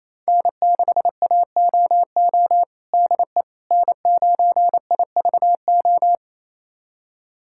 Morse code, heard as a keyed tone, about 28 wpm, 700 Hz. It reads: N6AOO DE N9IVO
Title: USA Callsign DE Callsign - 5 Characters - 28wpm